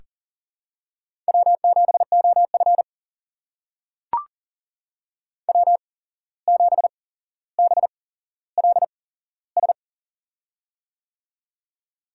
Morse code, heard as a keyed tone, about 42 wpm, spaced as for 16 wpm, 700 Hz. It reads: W7OF W7BLS
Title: USA Call Signs - 8x Character Spacing - 40wpm